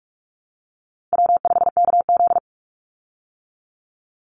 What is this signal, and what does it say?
Morse code, heard as a keyed tone, about 44 wpm, 690 Hz. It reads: W5KZ
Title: USA Call Signs 45wpm